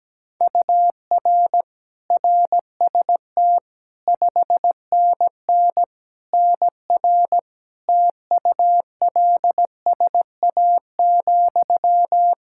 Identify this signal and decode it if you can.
Morse code, heard as a keyed tone, about 17 wpm, 700 Hz. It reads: UR RST 5NN NR TULSA,
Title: Practice QSO - Exchange 1 - Beginning - 17wpm